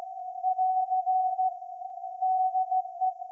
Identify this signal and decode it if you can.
Morse code, heard as a keyed tone, about 16 wpm, 750 Hz. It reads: <AR> DE
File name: iz7slz / VLF / SAQ 30-06-2024 0900